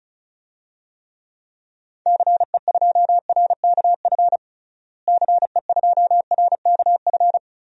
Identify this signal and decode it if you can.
Morse code, heard as a keyed tone, about 34 wpm, 700 Hz. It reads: CE2RKF CE2RKF
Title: DXCC-100 Call Signs - 6 Characters - Repeated 2x - 35wpm